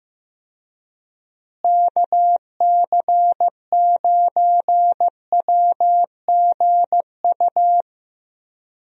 Morse code, heard as a keyed tone, about 15 wpm, 700 Hz. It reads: KC9WGU
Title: USA Call Signs - 6 Characters - Repeated 3x - 15wpm